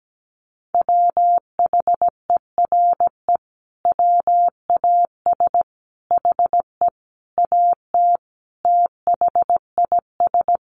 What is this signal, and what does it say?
Morse code, heard as a keyed tone, about 17 wpm, 700 Hz. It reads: WHERE WAS HE AT THIS
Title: Sentences from Top 1000 Words 17wpm